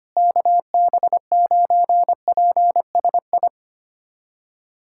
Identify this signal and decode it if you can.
Morse code, heard as a keyed tone, about 25 wpm, 700 Hz. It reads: KB9PSI
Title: USA Callsign DE Callsign - 6 Characters - 25wpm